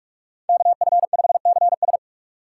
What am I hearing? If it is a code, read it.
Morse code, heard as a keyed tone, about 46 wpm, 700 Hz. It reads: KF5CS
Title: USA Call Signs - 5 Characters - 2x Character Spacing - 45wpm